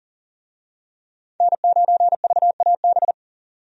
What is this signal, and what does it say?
Morse code, heard as a keyed tone, about 37 wpm, 700 Hz. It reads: N9VAB
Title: USA Call Signs - 5 Characters - 40wpm